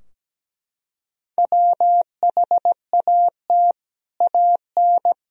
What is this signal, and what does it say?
Morse code, heard as a keyed tone, about 17 wpm, 700 Hz. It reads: WHAT AN
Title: Sentences from Top 1000 Words - Review Part 3 - 17wpm